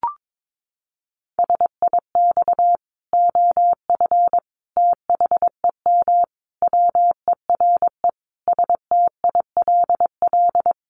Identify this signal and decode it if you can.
Morse code, heard as a keyed tone, about 22 wpm, 700 Hz. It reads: SIX OF THEM WERE STILL
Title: Sentences from Top 1100 Words 22wpm